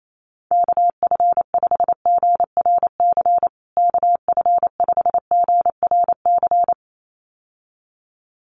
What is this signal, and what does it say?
Morse code, heard as a keyed tone, about 28 wpm, 700 Hz. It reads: KF5GRC KF5GRC
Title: USA Call Signs Repeated 2x 28wpm